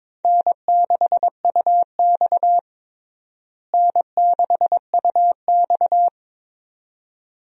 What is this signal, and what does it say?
Morse code, heard as a keyed tone, about 22 wpm, 700 Hz. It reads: N6UX N6UX
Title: USA Call Signs - 4 Characters - Repeated 2x - 22wpm